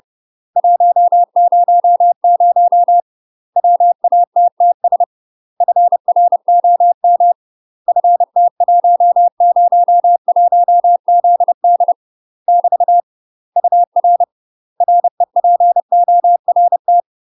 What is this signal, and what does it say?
Morse code, heard as a keyed tone, about 30 wpm, 700 Hz. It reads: 100 WATTS FROM FT101ZD <BT> UR REPORT
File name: UK MASTER QSOs TRAINING / 30WPM / UK MASTER QSOs 0023